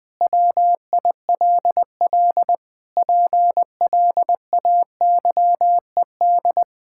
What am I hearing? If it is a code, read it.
Morse code, heard as a keyed tone, about 20 wpm, 700 Hz. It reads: WILL PLAYED
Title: Pangram Challenge v2 - 20wpm